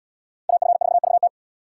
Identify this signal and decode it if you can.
Morse code, heard as a keyed tone, about 73 wpm, 700 Hz. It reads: AF4VT